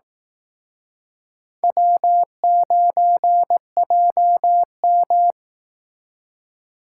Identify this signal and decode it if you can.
Morse code, heard as a keyed tone, about 18 wpm, 700 Hz. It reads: W9JM
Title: Speed Racing - USA 1x2 Call Signs 15wpm